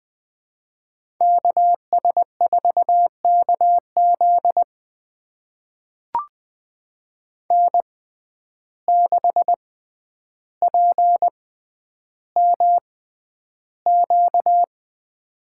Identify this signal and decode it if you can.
Morse code, heard as a keyed tone, about 20 wpm, spaced as for 10 wpm, 700 Hz. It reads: KS4KZ N6PMQ
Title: USA Call Signs - 5 Characters - 6x Character Spacing - 20wpm